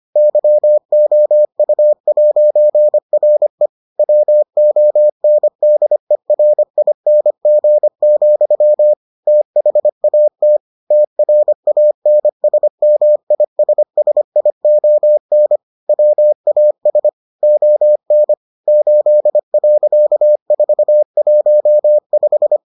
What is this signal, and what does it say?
Morse code, heard as a keyed tone, about 25 wpm, 600 Hz. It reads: YOU'RE WONDERING, THAT TRANSMISSION WAS ON 8.415